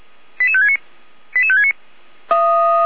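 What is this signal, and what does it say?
Five-tone selective call (ZVEI), heard as quick runs of five tones, tones of about 70 ms. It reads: 88569 88569
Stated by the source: ZVEI five-tone selective call received on an SDR.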